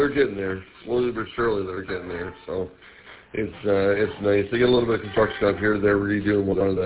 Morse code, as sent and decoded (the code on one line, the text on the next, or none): none